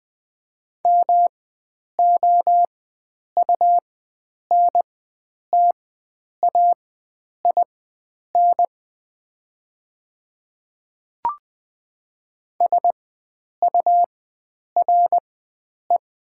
Morse code, heard as a keyed tone, about 20 wpm, spaced as for 9 wpm, 700 Hz. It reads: MOUNTAIN SURE
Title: 4x Character Spacing - Top 300 Words - 20wpm